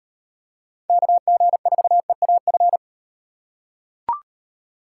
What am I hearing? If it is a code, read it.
Morse code, heard as a keyed tone, about 38 wpm, 700 Hz. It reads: KG4EAF